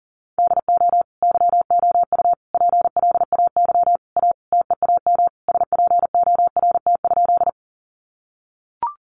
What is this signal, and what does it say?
Morse code, heard as a keyed tone, about 40 wpm, 700 Hz. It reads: DO YOU PLAY A TEAM SPORT?
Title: Sentences from Top 1500 Words 40wpm